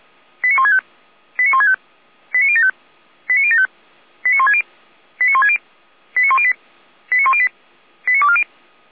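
Five-tone selective call (ZVEI), heard as quick runs of five tones, tones of about 80 ms. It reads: 89165 89165 89075 89075 89160 89160 89108 89108 89240
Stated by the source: ZVEI five-tone selective call received by SDR, logged on a Feuerwehr Orlamünde fire brigade server.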